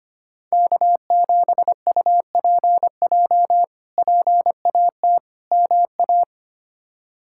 Morse code, heard as a keyed tone, about 25 wpm, 700 Hz. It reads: K7UPJ PAT MA
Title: K1USN SST QSO Practice - 25wpm